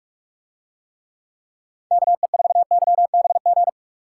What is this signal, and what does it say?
Morse code, heard as a keyed tone, about 40 wpm, 700 Hz. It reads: KE4YBG